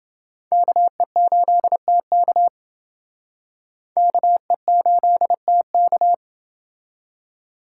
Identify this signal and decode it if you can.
Morse code, heard as a keyed tone, about 29 wpm, 700 Hz. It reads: KE8TK KE8TK